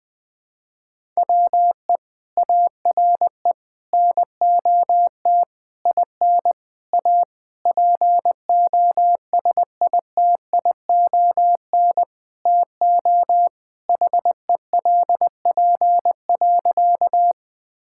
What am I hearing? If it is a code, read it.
Morse code, heard as a keyed tone, about 19 wpm, 700 Hz. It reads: WE ARE NOT IN A POSITION TO HELP.
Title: Sentences with 7-Word Phrases - Spoken Beginning - 20wpm